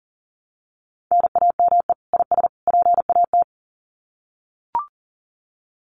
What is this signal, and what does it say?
Morse code, heard as a keyed tone, about 40 wpm, 700 Hz. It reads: NAME IS PAT
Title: Common QSO Phrases 40wpm